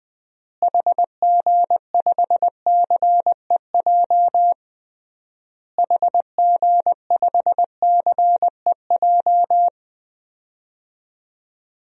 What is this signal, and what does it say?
Morse code, heard as a keyed tone, about 20 wpm, 700 Hz. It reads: HG5CEJ HG5CEJ
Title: DXCC-100 Call Signs - 6 Characters - Repeated 3x - 20wpm